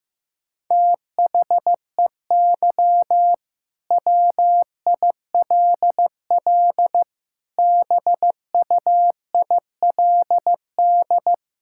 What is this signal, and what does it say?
Morse code, heard as a keyed tone, about 16 wpm, 700 Hz. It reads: THEY WILL BUILD
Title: Sentences from Top 1600 Words 15wpm